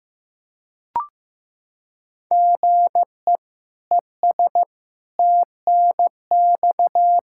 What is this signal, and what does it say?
Morse code, heard as a keyed tone, about 15 wpm, 700 Hz. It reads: GE ES TNX
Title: Practice QSO - Exchange 1 - Beginning - 15wpm